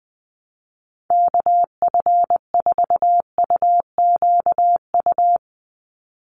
Morse code, heard as a keyed tone, about 20 wpm, 700 Hz. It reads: KF4UQU
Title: USA Call Signs 20wpm